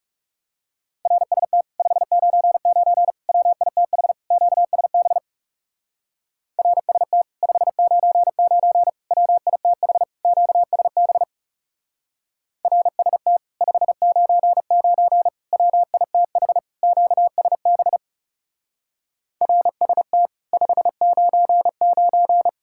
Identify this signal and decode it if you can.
Morse code, heard as a keyed tone, about 35 wpm, 700 Hz. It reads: RST 599 WITH QSB RST 599 WITH QSB RST 599 WITH QSB RST 599
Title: Speed-Racing Common QSO Phrases - 30wpm